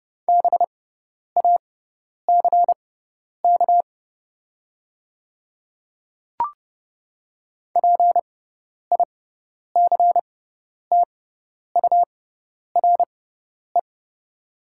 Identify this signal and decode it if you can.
Morse code, heard as a keyed tone, about 30 wpm, spaced as for 10 wpm, 700 Hz. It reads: BACK PICTURE
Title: 6x Character Spacing - Top 300 Words - 30wpm